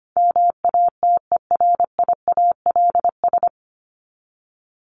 Morse code, heard as a keyed tone, about 25 wpm, 700 Hz. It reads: MATERIALS
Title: ES:S Suffix - Top 100 Words with Base Words 25wpm